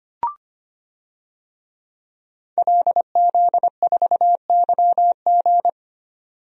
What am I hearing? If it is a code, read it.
Morse code, heard as a keyed tone, about 25 wpm, 700 Hz. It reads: LZ4YG
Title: DXCC-100 Call Signs - 5 Characters - 25wpm